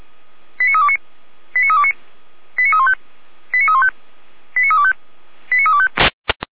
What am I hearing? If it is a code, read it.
Five-tone selective call (ZVEI), heard as quick runs of five tones, tones of about 70 ms. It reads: 89329 89329 89315 89315 89325 89325